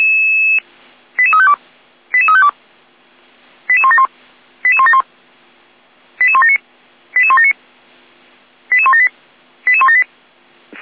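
Five-tone selective call (ZVEI), heard as quick runs of five tones, tones of about 70 ms. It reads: 88352 88352 88171 88171 88179 88179 88178 88178
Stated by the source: ZVEI five-tone selective call received via SDR.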